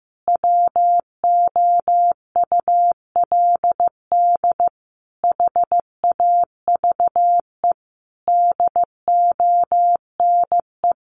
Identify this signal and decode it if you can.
Morse code, heard as a keyed tone, about 15 wpm, 700 Hz. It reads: WOULD HAVE DONE
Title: Sentences from Top 1100 Words 15wpm